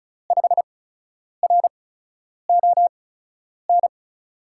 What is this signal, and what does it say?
Morse code, heard as a keyed tone, about 36 wpm, spaced as for 10 wpm, 700 Hz. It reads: 5RON
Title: USA Call Signs - 6 Characters - 8x Character Spacing - 35wpm